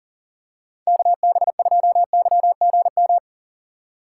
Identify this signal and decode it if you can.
Morse code, heard as a keyed tone, about 37 wpm, 700 Hz. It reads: KB2YGM
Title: USA Call Signs - 6 Characters - 4x Character Spacing - 40wpm